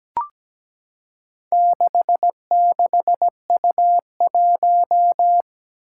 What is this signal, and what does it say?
Morse code, heard as a keyed tone, about 17 wpm, 700 Hz. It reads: 66U1
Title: Four Challenging Letter-Number Combinations 17wpm